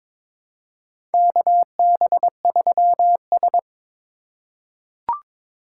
Morse code, heard as a keyed tone, about 22 wpm, 700 Hz. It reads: KB3S
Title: USA Call Signs - 3x Character Spacing - 22wpm